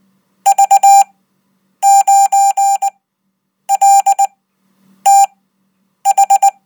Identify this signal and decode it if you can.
Morse code, heard as a keyed tone, about 19 wpm, spaced as for 8 wpm, 790 Hz. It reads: V9LTH
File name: Code